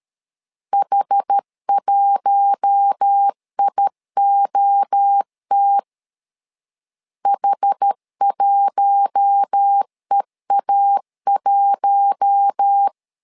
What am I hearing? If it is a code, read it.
Morse code, heard as a keyed tone, about 13 wpm, 800 Hz. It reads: H1IOT H1EA1
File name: LEZIONI / 4 Q1 12wpm 2028